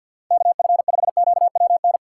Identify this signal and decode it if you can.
Morse code, heard as a keyed tone, about 52 wpm, 700 Hz. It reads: KF5QWN
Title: USA Call Signs - 6 Characters - 3x Character Spacing - 50wpm